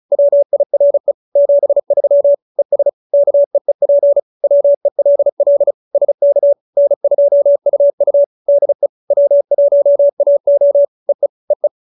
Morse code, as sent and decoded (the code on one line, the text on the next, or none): .-- .. .-. / --... ...-- / . ... / -.- . .--. / .-- . .-.. .-.. / ... -.- / -. ..--- ..- ..- / -.. / .-- .---- .- --- / . / . .
WIRE 73 ES KEEP WELL SK N2UU DE W1AO EE EE